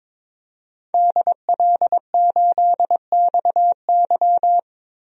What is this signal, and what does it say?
Morse code, heard as a keyed tone, about 22 wpm, 700 Hz. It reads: DL8XY
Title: DXCC-100 Call Signs - 5 Characters - 22wpm